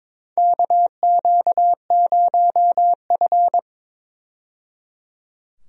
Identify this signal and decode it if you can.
Morse code, heard as a keyed tone, about 22 wpm, 700 Hz. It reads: KQ0F
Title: USA Call Signs - 4 Characters - 22wpm